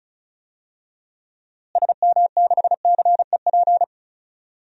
Morse code, heard as a keyed tone, about 34 wpm, 700 Hz. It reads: SM6CEP
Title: DXCC-100 Call Signs - 6 Characters - Repeated 3x - 35wpm